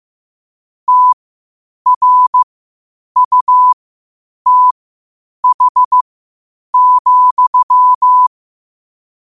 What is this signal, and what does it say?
Morse code, heard as a keyed tone, about 15 wpm, spaced as for 8 wpm, 1000 Hz. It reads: TRUTH,
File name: qotdfiles / 8wpm